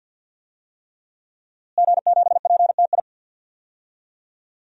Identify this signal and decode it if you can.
Morse code, heard as a keyed tone, about 46 wpm, 700 Hz. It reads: G7PTI
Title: DXCC-100 Call Signs - 5 Characters - Repeated 3x - 50wpm